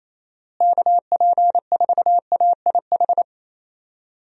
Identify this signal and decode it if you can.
Morse code, heard as a keyed tone, about 28 wpm, 700 Hz. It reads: KP4AIH
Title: DXCC-100 Call Signs - 6 Characters - Repeated 2x - 28wpm